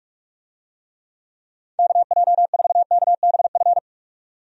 Morse code, heard as a keyed tone, about 46 wpm, 700 Hz. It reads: KJ4KBF
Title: USA Call Signs - 6 Characters - 45wpm